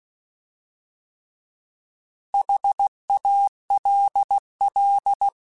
Morse code, heard as a keyed tone, about 16 wpm, 800 Hz. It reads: HALL
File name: Alphabetical / Alphabetical mp3s / Hs @15